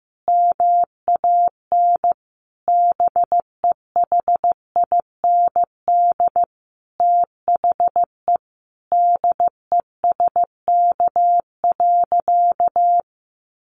Morse code, heard as a keyed tone, about 15 wpm, 700 Hz. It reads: MAN BEHIND THE DESK.
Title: A Sound Of Thunder 15wpm